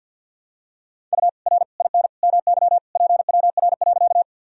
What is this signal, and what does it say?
Morse code, heard as a keyed tone, about 52 wpm, 700 Hz. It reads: U R IN MY PWR.
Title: Sentences from Top 500 Words and Abbreviations 50wpm